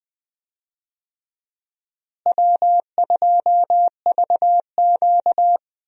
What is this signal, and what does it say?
Morse code, heard as a keyed tone, about 20 wpm, 700 Hz. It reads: W2VQ
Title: USA Call Signs - 4 Characters - Repeated 3x - 20wpm